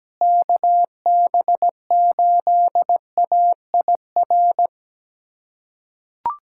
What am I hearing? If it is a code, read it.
Morse code, heard as a keyed tone, about 17 wpm, 700 Hz. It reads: KB8AIR
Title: USA Callsign DE Callsign 17wpm